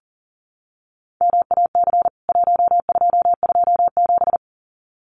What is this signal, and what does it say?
Morse code, heard as a keyed tone, about 40 wpm, 700 Hz. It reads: MAC 1227
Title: CWT Exchange 40wpm